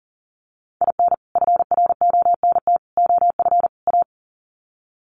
Morse code, heard as a keyed tone, about 40 wpm, 700 Hz. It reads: IN FRONT OF A
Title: Sets of 4 Words 40wpm